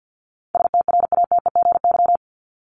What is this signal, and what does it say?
Morse code, heard as a keyed tone, about 50 wpm, 700 Hz. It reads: STRATEGY
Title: Top 501-750 Words 50wpm